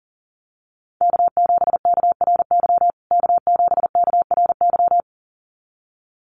Morse code, heard as a keyed tone, about 37 wpm, 700 Hz. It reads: K7KRY K7KRY
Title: USA Call Signs Repeated 2x 40wpm